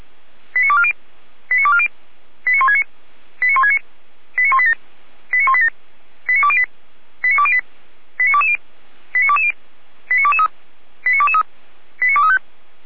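Five-tone selective call (ZVEI), heard as quick runs of five tones, tones of about 70 ms. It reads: 89240 89240 89169 89169 89187 89187 89208 89208 89220 89220 89223 89223 89235
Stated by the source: ZVEI five-tone call received by SDR, from the server of the Orlamünde fire brigade (feuerwehr-orlamuende).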